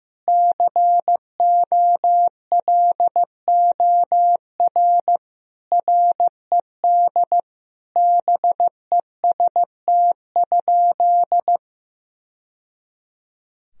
Morse code, heard as a keyed tone, about 15 wpm, 700 Hz. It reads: COLOR RED BEST?
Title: Sentences from Top 1000 Words - Review Part 2 - 15wpm